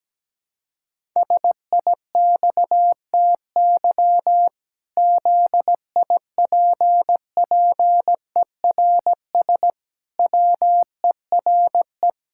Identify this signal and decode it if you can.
Morse code, heard as a keyed tone, about 17 wpm, 700 Hz. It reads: SIXTY ZIPPERS WERE
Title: Pangram Challenge v2 - 17wpm